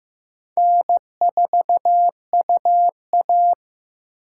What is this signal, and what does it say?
Morse code, heard as a keyed tone, about 15 wpm, 700 Hz. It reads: N4UA
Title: USA Call Signs - 4 Characters - 3x Character Spacing - 15wpm